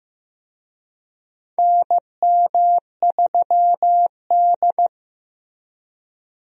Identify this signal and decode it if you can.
Morse code, heard as a keyed tone, about 15 wpm, 700 Hz. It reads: NM3D
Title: USA Call Signs - 4 Characters - 15wpm